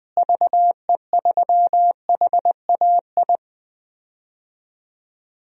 Morse code, heard as a keyed tone, about 20 wpm, 700 Hz. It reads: VE3HAI
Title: DXCC-100 Call Signs - 6 Characters - Repeated 2x - 20wpm